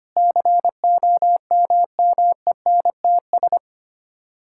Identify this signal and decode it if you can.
Morse code, heard as a keyed tone, about 25 wpm, 700 Hz. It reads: COMMENTS
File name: Sets of 9 Words - 25wpm